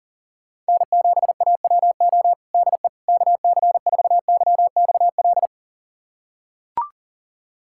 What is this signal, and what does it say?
Morse code, heard as a keyed tone, about 40 wpm, 700 Hz. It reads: N7AWO DE KC4YXL
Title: USA Callsign DE Callsign 40wpm